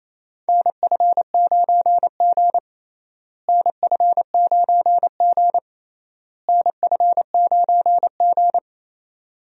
Morse code, heard as a keyed tone, about 28 wpm, 700 Hz. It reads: NF9G NF9G NF9G
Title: USA Call Signs - 4 Characters - Repeated 3x - 28wpm